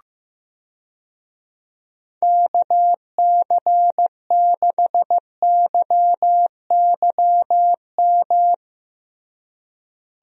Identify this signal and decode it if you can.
Morse code, heard as a keyed tone, about 16 wpm, 700 Hz. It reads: KC6YYM
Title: USA Call Signs - 6 Characters - Repeated 2x - 15wpm